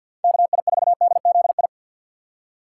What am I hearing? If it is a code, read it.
Morse code, heard as a keyed tone, about 53 wpm, 700 Hz. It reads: KI4DZI
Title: USA Call Signs - 6 Characters - 3x Character Spacing - 50wpm